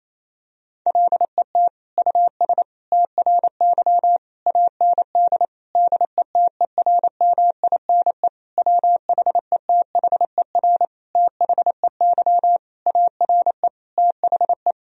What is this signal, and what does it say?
Morse code, heard as a keyed tone, about 28 wpm, 700 Hz. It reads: LET US TRY AND DETERMINE WHETHER THEY ARE THE
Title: Sentences from Top 1000 Words - Review Part 4 - 28wpm